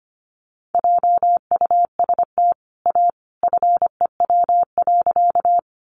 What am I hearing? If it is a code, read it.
Morse code, heard as a keyed tone, about 25 wpm, 700 Hz. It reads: JUST A FEW.
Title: Sentences from Top 300 Words 25wpm